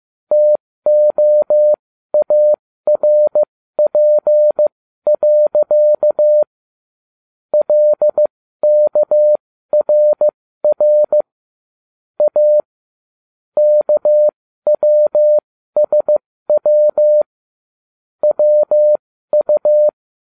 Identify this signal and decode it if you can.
Morse code, heard as a keyed tone, about 15 wpm, spaced as for 13 wpm, 600 Hz. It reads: TOARP. LKRR A KWSW WU